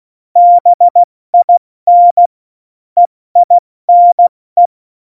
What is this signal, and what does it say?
Morse code, heard as a keyed tone, about 16 wpm, 700 Hz. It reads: BIN EINE
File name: Morsekurs / Witze / Witz 17. Oktober 17, Tempo 70